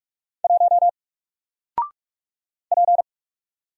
Morse code, heard as a keyed tone, about 45 wpm, 700 Hz. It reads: 1 P
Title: Single Letter-Number Alt 45wpm